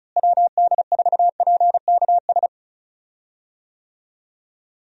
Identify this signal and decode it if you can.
Morse code, heard as a keyed tone, about 36 wpm, 700 Hz. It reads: WD4PKS
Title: USA Call Signs - 6 Characters - Repeated 3x - 35wpm